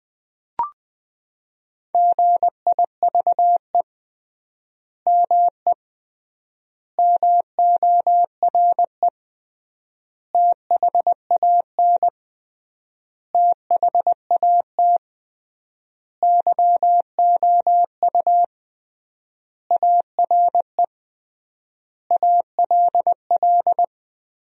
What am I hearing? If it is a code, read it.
Morse code, heard as a keyed tone, about 21 wpm, 700 Hz. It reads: GIVE ME MORE THAN THAT YOU ARE ALL
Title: 3x Word-Spacing - Sentences from Top 200 Words - 20wpm